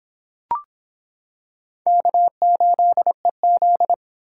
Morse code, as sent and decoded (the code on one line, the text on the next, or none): -.- ---.. --..
K8EZ